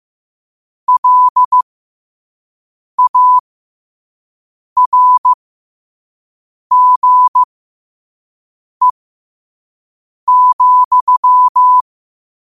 Morse code, heard as a keyed tone, about 15 wpm, spaced as for 5 wpm, 1000 Hz. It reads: LARGE,
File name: qotdfiles / 5wpm